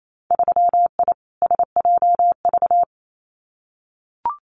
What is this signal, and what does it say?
Morse code, heard as a keyed tone, about 28 wpm, 700 Hz. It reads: 3I SJV
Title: Winter Field Day Exchanges Encore 28wpm